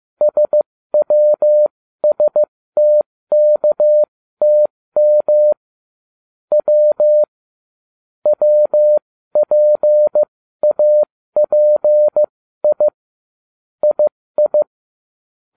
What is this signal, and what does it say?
Morse code, heard as a keyed tone, about 15 wpm, spaced as for 13 wpm, 600 Hz. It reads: SWSTKTM W WPAPI II